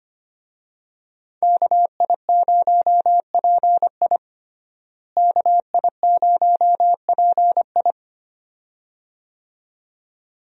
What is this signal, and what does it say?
Morse code, heard as a keyed tone, about 25 wpm, 700 Hz. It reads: KI0PI KI0PI
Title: USA Call Signs - 5 Characters - Repeated 2x - 25wpm